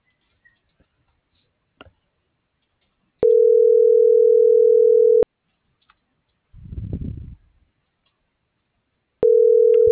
Ringback tone (North America), 440 + 480 Hz.